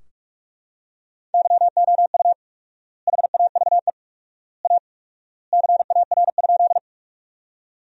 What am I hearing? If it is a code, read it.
Morse code, heard as a keyed tone, about 45 wpm, 700 Hz. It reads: YOU HAVE A CAR?
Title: 4x Word-Spacing - Sentences from Top 300 Words - 45wpm